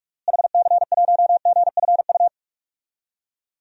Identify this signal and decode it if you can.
Morse code, heard as a keyed tone, about 46 wpm, 700 Hz. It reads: HC1GFU
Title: DXCC-100 Call Signs - 6 Characters - Repeated 3x - 45wpm